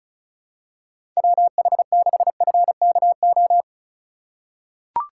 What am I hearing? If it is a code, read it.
Morse code, heard as a keyed tone, about 36 wpm, 700 Hz. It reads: WH6FKO
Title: USA Call Signs - 6 Characters - 2x Character Spacing - 35wpm